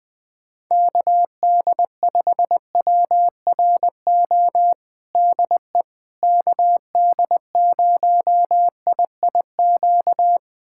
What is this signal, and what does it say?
Morse code, heard as a keyed tone, about 20 wpm, 700 Hz. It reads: KD5WRO DE KD0IIQ
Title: USA Callsign DE Callsign - 6 Characters - 20wpm